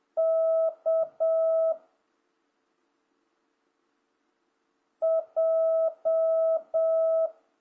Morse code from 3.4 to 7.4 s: J